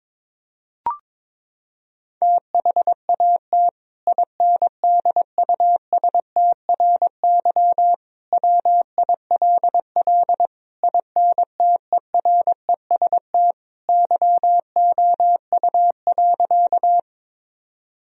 Morse code, heard as a keyed tone, about 22 wpm, 700 Hz. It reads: THAT INDUSTRY WILL INTEREST YOU.
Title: Sentences from Top 1000 Words - Review Part 4 - 22wpm